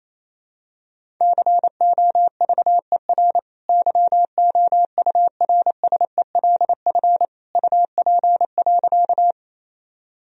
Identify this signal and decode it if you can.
Morse code, heard as a keyed tone, about 28 wpm, 700 Hz. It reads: COVER YOURSELF UP.
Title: Sentences from Top 1000 Words - Review Part 1 - 28wpm